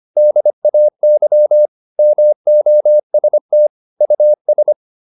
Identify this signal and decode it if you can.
Morse code, heard as a keyed tone, about 25 wpm, 600 Hz. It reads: DAY MOST US
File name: Top-100-Words-25